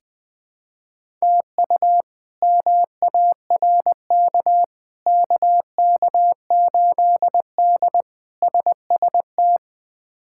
Morse code, heard as a keyed tone, about 21 wpm, 700 Hz. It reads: TU MARK KK8D SST
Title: K1USN SST QSO Practice - 20wpm